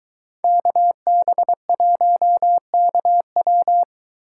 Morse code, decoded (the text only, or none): KB1KW